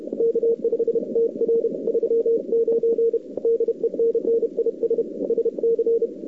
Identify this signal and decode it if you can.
Morse code, heard as a keyed tone, about 33 wpm, 440 Hz. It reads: K5TR 39 DENNIS SC
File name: CWSprint / Feb2007 / 2007-feb-cw-sprint-k5tr-op-n5rz -01:00